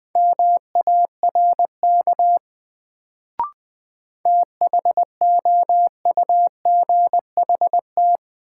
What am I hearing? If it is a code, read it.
Morse code, heard as a keyed tone, about 20 wpm, 700 Hz. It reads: MARK THOUGHT
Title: Sentences from Top 300 Words - New Word Drill - 20wpm